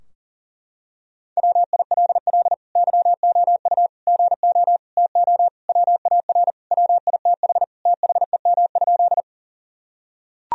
Morse code, heard as a keyed tone, about 40 wpm, 700 Hz. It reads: WILL YOU GO TO WAR WITH THEM?
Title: Sentences from Top 1000 Words - Review Part 2 - 40wpm